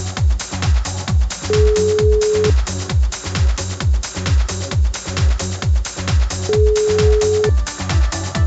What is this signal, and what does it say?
Ringback tone (Europe, 425 Hz).